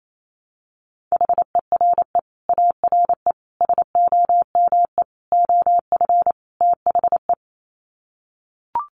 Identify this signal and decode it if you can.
Morse code, heard as a keyed tone, about 28 wpm, 700 Hz. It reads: HERE ARE SOME OF THE
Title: Sets of 5 Words Encore 28wpm